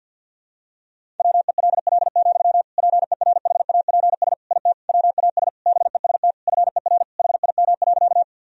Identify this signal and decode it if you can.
Morse code, heard as a keyed tone, about 50 wpm, 700 Hz. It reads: WELL, PERHAPS IT WAS BEST FER HIM.
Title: Sentences from Top 500 Words and Abbreviations 50wpm